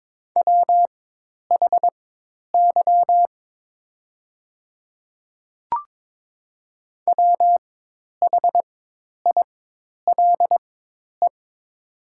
Morse code, heard as a keyed tone, about 22 wpm, spaced as for 10 wpm, 700 Hz. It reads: WHY WHILE
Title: 4x Character Spacing - Top 300 Words - 22wpm